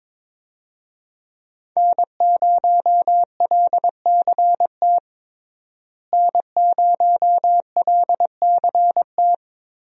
Morse code, heard as a keyed tone, about 21 wpm, 700 Hz. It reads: N0LCT N0LCT
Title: USA Call Signs - 5 Characters - Repeated 2x - 22wpm